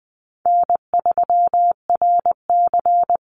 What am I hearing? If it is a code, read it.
Morse code, heard as a keyed tone, about 20 wpm, 700 Hz. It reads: N3RC
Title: Easy Call Signs Repeated 2x - 20wpm